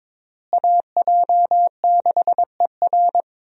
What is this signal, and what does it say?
Morse code, heard as a keyed tone, about 22 wpm, 700 Hz. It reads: AJ6ER
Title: USA Call Signs - 5 Characters - Repeated 3x - 22wpm